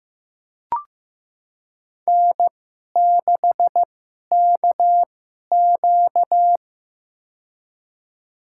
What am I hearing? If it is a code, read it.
Morse code, heard as a keyed tone, about 15 wpm, spaced as for 11 wpm, 700 Hz. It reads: N6KQ
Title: USA Call Signs - 2x Character Spacing - 15wpm